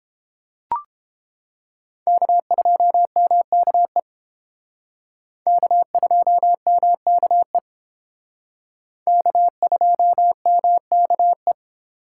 Morse code, heard as a keyed tone, about 28 wpm, 700 Hz. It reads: K2MKE K2MKE K2MKE